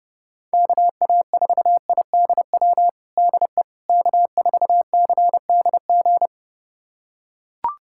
Morse code, heard as a keyed tone, about 32 wpm, 700 Hz. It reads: KA4IDW DE K4CDG
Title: USA Callsign DE Callsign 30wpm